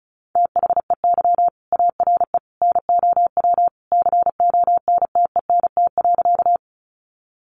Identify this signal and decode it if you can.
Morse code, heard as a keyed tone, about 34 wpm, 700 Hz. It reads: THEY ARE NOW CONTENT.
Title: Sentences from Top 900 Words 35wpm